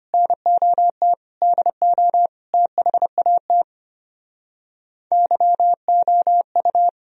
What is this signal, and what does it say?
Morse code, heard as a keyed tone, about 32 wpm, 700 Hz. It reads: NOT DO THAT YOU